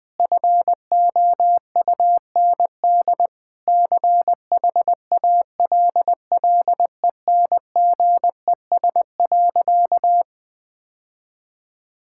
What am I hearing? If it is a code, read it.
Morse code, heard as a keyed tone, about 21 wpm, 700 Hz. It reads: FOUND CHALLENGES.
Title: Sentences with 6-Word Phrases - Spoken Beginning - 20wpm